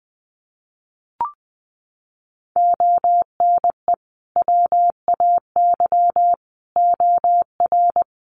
Morse code, heard as a keyed tone, about 20 wpm, 700 Hz. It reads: ONE WAY OR
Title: Sets of 5 Words 20wpm